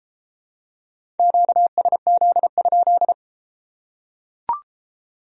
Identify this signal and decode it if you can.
Morse code, heard as a keyed tone, about 31 wpm, 700 Hz. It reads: QSZ?